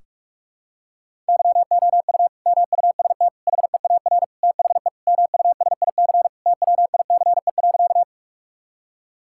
Morse code, heard as a keyed tone, about 46 wpm, 700 Hz. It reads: YOU MUST HEAR THE MUSIC TWICE.
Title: Sentences from Top 1600 Words 45wpm